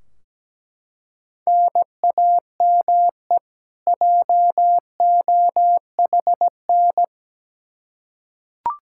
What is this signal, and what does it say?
Morse code, heard as a keyed tone, about 17 wpm, 700 Hz. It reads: NAME JOHN
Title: Practice QSO - Exchange 1 - Beginning - 17wpm